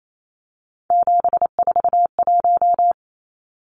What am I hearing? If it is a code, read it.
Morse code, heard as a keyed tone, about 28 wpm, 700 Hz. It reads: 741
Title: Numbers - 3-Digit 28wpm